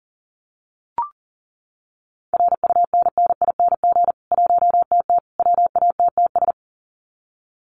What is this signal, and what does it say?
Morse code, heard as a keyed tone, about 40 wpm, 700 Hz. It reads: RUNNING 1TT WATTS
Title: Common QSO Phrases 40wpm